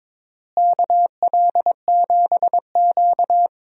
Morse code, heard as a keyed tone, about 22 wpm, 700 Hz. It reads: KL7Q
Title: DXCC-100 Call Signs - 4 Characters - Repeated 3x - 22wpm